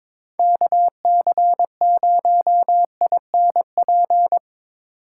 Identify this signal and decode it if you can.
Morse code, heard as a keyed tone, about 22 wpm, 700 Hz. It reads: KC0INP
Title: K1USN SST QSO Practice - 22wpm